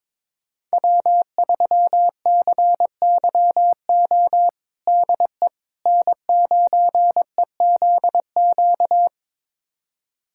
Morse code, heard as a keyed tone, about 22 wpm, 700 Hz. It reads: W3CYO DE N9EZQ
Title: USA Callsign DE Callsign - 5 Characters - 22wpm